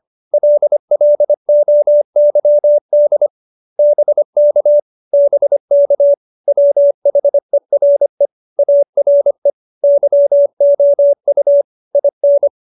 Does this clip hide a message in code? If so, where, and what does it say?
Yes: Morse code from 0.3 to 12.5 s, heard as a keyed tone, about 25 wpm, 590 Hz. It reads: LLOYD BK BK WHERE ARE YOU IN